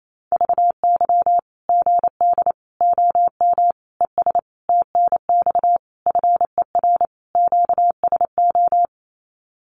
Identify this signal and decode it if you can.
Morse code, heard as a keyed tone, about 28 wpm, 700 Hz. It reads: VY GD OM ES TNX FER QSO
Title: Common QSO Phrases 28wpm